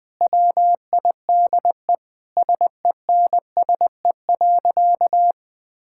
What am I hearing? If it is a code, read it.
Morse code, heard as a keyed tone, about 21 wpm, 700 Hz. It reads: WIDE SENSE.
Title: Sentences from Top 1000 Words - Review Part 6 - 20wpm